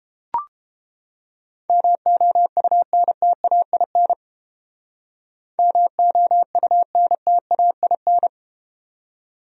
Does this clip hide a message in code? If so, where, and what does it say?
Yes: Morse code from 1.7 to 8.3 s, heard as a keyed tone, about 31 wpm, 700 Hz. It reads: MOUNTAIN MOUNTAIN